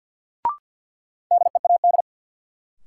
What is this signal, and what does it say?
Morse code, heard as a keyed tone, about 50 wpm, 700 Hz. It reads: DEAD
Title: Sentences from Top 1500 Words - New Word Drill Alt - 50wpm